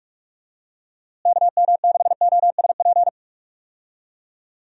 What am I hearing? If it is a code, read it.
Morse code, heard as a keyed tone, about 46 wpm, 700 Hz. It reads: KM6OSP
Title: USA Call Signs - 6 Characters - Repeated 2x - 45wpm